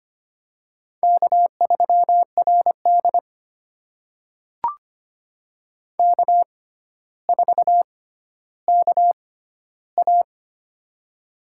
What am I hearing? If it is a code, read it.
Morse code, heard as a keyed tone, about 25 wpm, spaced as for 13 wpm, 700 Hz. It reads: K3RD K4KA
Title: USA Call Signs - 4 Characters - 6x Character Spacing - 25wpm